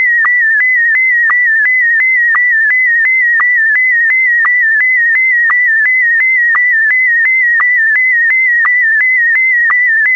SSTV (slow-scan television), Scottie DX.